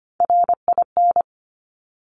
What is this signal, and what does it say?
Morse code, heard as a keyed tone, about 25 wpm, 700 Hz. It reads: RIN
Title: Top Three Letter Combinations in Words 25wpm